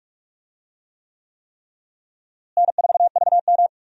Morse code, heard as a keyed tone, about 46 wpm, 700 Hz. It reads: N4VM
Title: USA Call Signs - 4 Characters - Repeated 3x - 45wpm